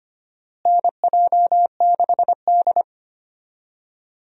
Morse code, heard as a keyed tone, about 25 wpm, 700 Hz. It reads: NJ6D
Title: USA Call Signs - 4 Characters - Repeated 2x - 25wpm